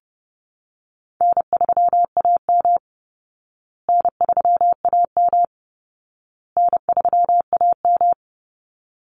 Morse code, heard as a keyed tone, about 30 wpm, 700 Hz. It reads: N3AM N3AM N3AM
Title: Easy Call Signs Repeated 3x - 30wpm